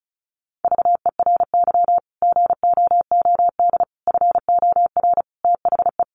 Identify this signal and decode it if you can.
Morse code, heard as a keyed tone, about 34 wpm, 700 Hz. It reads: VERY GOOD FOR THE
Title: Sentences from Top 1100 Words 35wpm